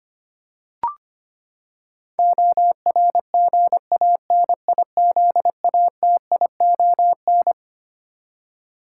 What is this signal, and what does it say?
Morse code, heard as a keyed tone, about 25 wpm, 700 Hz. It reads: ORGANIZATION
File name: Twelve Letter Words - 25wpm